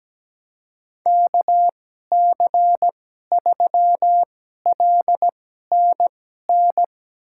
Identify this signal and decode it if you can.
Morse code, heard as a keyed tone, about 17 wpm, spaced as for 12 wpm, 700 Hz. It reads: KC3LNN
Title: USA Call Signs - 2x Character Spacing - 17wpm